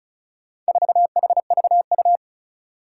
Morse code, heard as a keyed tone, about 36 wpm, 690 Hz. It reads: 4HVU
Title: Four Challenging Letter-Number Combinations 35wpm